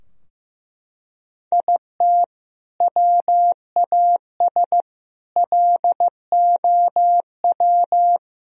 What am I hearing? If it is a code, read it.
Morse code, heard as a keyed tone, about 15 wpm, 700 Hz. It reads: IT WAS LOW
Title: Sentences from Top 1000 Words - Review Part 2 - 15wpm